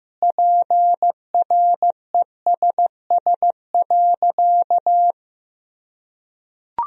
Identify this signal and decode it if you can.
Morse code, heard as a keyed tone, about 15 wpm, 700 Hz. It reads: PRESS.
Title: Sentences from Top 1000 Words - Review Part 1 - 15wpm